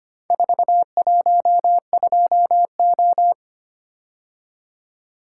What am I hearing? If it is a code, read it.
Morse code, heard as a keyed tone, about 25 wpm, 700 Hz. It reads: 412O
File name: practice / 25wpm / Four Letter-Number Combinations - 25wpm